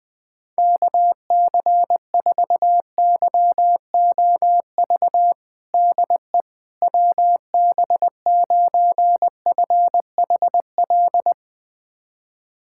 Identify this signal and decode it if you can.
Morse code, heard as a keyed tone, about 20 wpm, 700 Hz. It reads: KC4YOV DE WB9FHL
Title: USA Callsign DE Callsign - 6 Characters - 20wpm